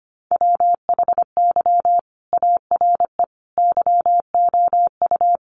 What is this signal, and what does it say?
Morse code, heard as a keyed tone, about 25 wpm, 700 Hz. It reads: WHY ARE YOU
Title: Sentences from Top 1400 Words 25wpm